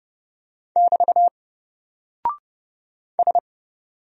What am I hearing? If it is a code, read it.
Morse code, heard as a keyed tone, about 31 wpm, 700 Hz. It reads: <BT> S
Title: Single Character (Letter:Number:Punctuation:Prosign) 30wpm